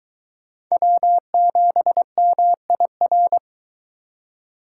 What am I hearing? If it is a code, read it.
Morse code, heard as a keyed tone, about 23 wpm, 700 Hz. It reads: W7MIR